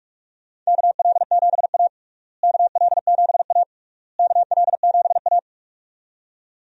Morse code, heard as a keyed tone, about 46 wpm, 700 Hz. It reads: KL7A KL7A KL7A
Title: DXCC-100 Call Signs - 4 Characters - Repeated 3x - 45wpm